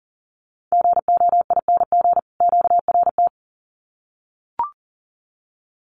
Morse code, heard as a keyed tone, about 40 wpm, 700 Hz. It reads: GOING QRT
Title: Common QSO Phrases 40wpm